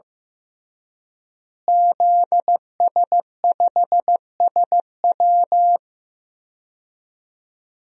Morse code, heard as a keyed tone, about 15 wpm, 700 Hz. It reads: ZS5SW
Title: DXCC-100 Call Signs - 5 Characters - Repeated 2x - 15wpm